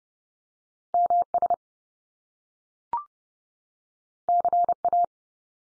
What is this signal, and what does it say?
Morse code, heard as a keyed tone, about 30 wpm, 700 Hz. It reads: MS CA
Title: US States 30WPM